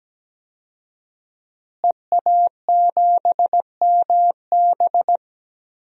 Morse code, heard as a keyed tone, about 17 wpm, 700 Hz. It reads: EA7MB
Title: DXCC-100 Call Signs - 5 Characters - Repeated 2x - 17wpm